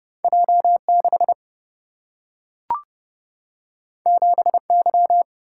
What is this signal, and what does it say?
Morse code, heard as a keyed tone, about 30 wpm, 700 Hz. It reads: J6 7Y
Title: Single-Letter Single-Number 30wpm